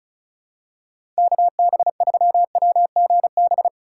Morse code, heard as a keyed tone, about 36 wpm, 700 Hz. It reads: KB3WGB
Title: USA Call Signs - 3x Character Spacing - 35wpm